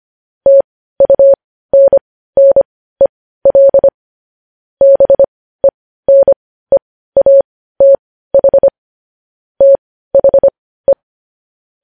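Morse code, heard as a keyed tone, about 25 wpm, spaced as for 15 wpm, 550 Hz. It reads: TUNNEL BENEATH THE
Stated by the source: Morse practice of news headlines at 15 wpm.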